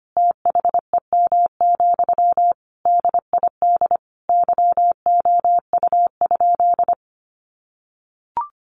Morse code, heard as a keyed tone, about 25 wpm, 700 Hz. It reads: THEM, DID YOU?
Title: Sentences from Top 1500 Words 25wpm